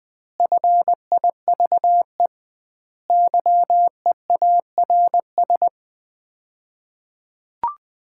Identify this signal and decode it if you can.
Morse code, heard as a keyed tone, about 20 wpm, 700 Hz. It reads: FIVE YEARS
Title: 2x Word-Spacing - Sets of 5 Words - 20wpm